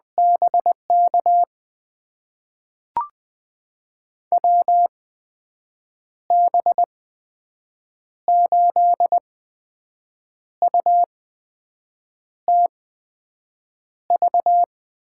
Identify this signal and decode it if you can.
Morse code, heard as a keyed tone, about 20 wpm, spaced as for 6 wpm, 700 Hz. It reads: BK WB8UTV
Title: USA Call Signs - 6 Characters - 8x Character Spacing - 20wpm